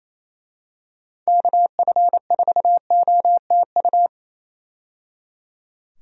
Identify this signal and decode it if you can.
Morse code, heard as a keyed tone, about 28 wpm, 700 Hz. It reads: KF4OTU
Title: USA Call Signs - 6 Characters - 28wpm